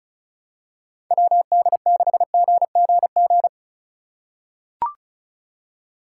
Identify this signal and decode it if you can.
Morse code, heard as a keyed tone, about 36 wpm, 700 Hz. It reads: WD6GGG
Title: USA Call Signs - 6 Characters - 2x Character Spacing - 35wpm